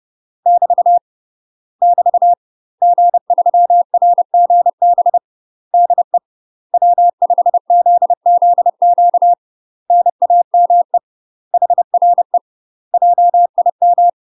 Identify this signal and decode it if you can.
Morse code, heard as a keyed tone, about 30 wpm, 700 Hz. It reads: <BT> <BT> G3RGB DE W5ZZQ NAME HRE JIM
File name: UK MASTER QSOs TRAINING / 30WPM / UK MASTER QSOs 0020